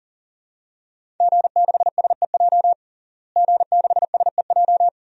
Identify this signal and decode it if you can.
Morse code, heard as a keyed tone, about 38 wpm, 700 Hz. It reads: G6SEJ G6SEJ
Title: DXCC-100 Call Signs - 5 Characters - Repeated 2x - 40wpm